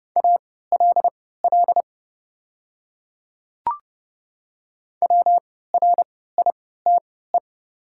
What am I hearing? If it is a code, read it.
Morse code, heard as a keyed tone, about 30 wpm, spaced as for 17 wpm, 700 Hz. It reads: ALL WRITE
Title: 3x Character Spacing - Top 200 Words - 30wpm